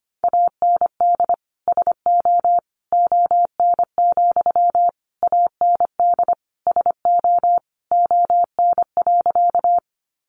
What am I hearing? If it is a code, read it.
Morse code, heard as a keyed tone, about 25 wpm, 700 Hz. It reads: AND SO ON, AND SO ON.
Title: Sentences from Top 100 Words 25wpm